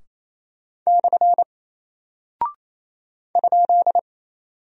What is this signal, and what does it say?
Morse code, heard as a keyed tone, about 28 wpm, 700 Hz. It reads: / ?
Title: Single Character (Letter:Number:Punctuation:Prosign) Alt 28wpm